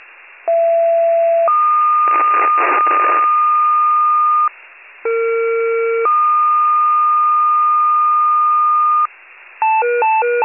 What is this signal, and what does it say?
Two-tone page, heard as a one-second tone, then a three-second tone, about 670 then 1220 Hz, about 460 then 1220 Hz. It is heard from 0.5 to 9.1 s.
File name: wcalert